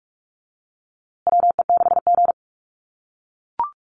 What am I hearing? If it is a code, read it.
Morse code, heard as a keyed tone, about 44 wpm, 690 Hz. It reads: WE6G
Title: USA Call Signs Repeated 3x 45wpm